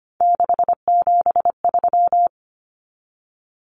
Morse code, heard as a keyed tone, about 25 wpm, 700 Hz. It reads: 673
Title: Common Cut Numbers - 3-Digit 25wpm